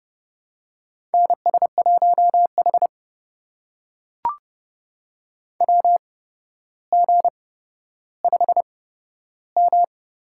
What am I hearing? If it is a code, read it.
Morse code, heard as a keyed tone, about 30 wpm, spaced as for 13 wpm, 700 Hz. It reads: NS1H WG5M
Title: USA Call Signs - 4 Characters - 8x Character Spacing - 30wpm